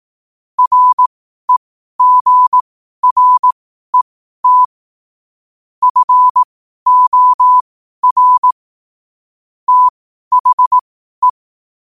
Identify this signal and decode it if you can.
Morse code, heard as a keyed tone, about 18 wpm, spaced as for 12 wpm, 1000 Hz. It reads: REGRET FOR THE
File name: qotdfiles / 12wpm